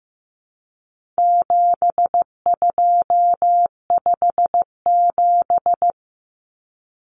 Morse code, heard as a keyed tone, about 15 wpm, 700 Hz. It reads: 7257
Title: Numbers - 4-Digit 15wpm